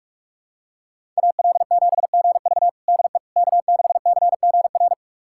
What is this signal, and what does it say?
Morse code, heard as a keyed tone, about 43 wpm, 700 Hz. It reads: AL7GV DE K6CGR
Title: USA Callsign DE Callsign - 5 Characters - 45wpm